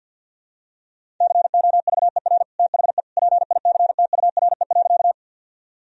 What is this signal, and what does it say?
Morse code, heard as a keyed tone, about 51 wpm, 700 Hz. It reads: COVER THE PICTURE.
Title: Sentences from Top 1000 Words - Review Part 1 - 50wpm